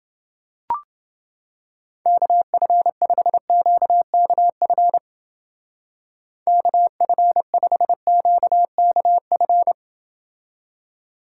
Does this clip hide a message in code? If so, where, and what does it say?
Yes: Morse code from 2.1 to 9.7 s, heard as a keyed tone, about 30 wpm, 700 Hz. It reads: KF5QKF KF5QKF